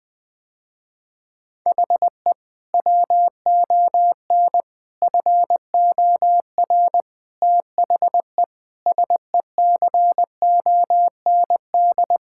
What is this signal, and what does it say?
Morse code, heard as a keyed tone, about 20 wpm, 700 Hz. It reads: HE WON FOR THE SECOND
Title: Sentences with 7-Word Phrases - Spoken Beginning - 20wpm